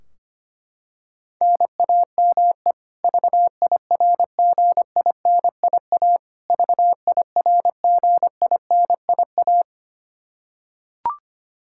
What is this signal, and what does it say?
Morse code, heard as a keyed tone, about 25 wpm, 700 Hz. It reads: NAME VIRGINIA VIRGINIA
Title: Practice QSO - Exchange 1 - Beginning - 25wpm